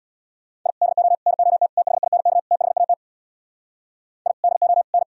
Morse code, heard as a keyed tone, about 77 wpm, 700 Hz. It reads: I DO NOT THINK THAT I DO N